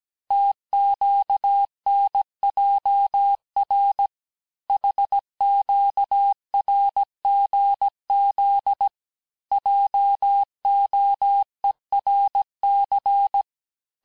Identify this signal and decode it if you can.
Morse code, heard as a keyed tone, about 17 wpm, 800 Hz. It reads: TQNJR HQRGZ JOERC
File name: cours-du-lundi files / corrige-cours-1581-texte-chiffre-vitesse-16-mpm